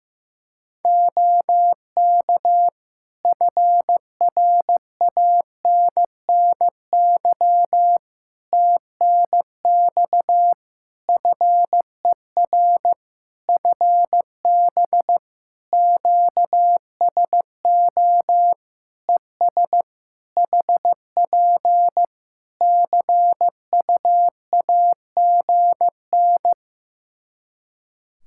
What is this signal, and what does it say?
Morse code, heard as a keyed tone, about 16 wpm, 700 Hz. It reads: OK FRANNY TNX FER FB QSO ES HP CUAGN
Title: Practice QSO - Exchange 4 - Ending - 15wpm